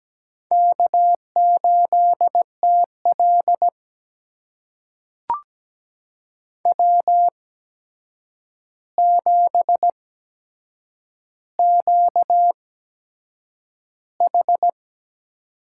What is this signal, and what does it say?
Morse code, heard as a keyed tone, about 17 wpm, spaced as for 7 wpm, 700 Hz. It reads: K8TL W7QH
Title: USA Call Signs - 4 Characters - 8x Character Spacing - 17wpm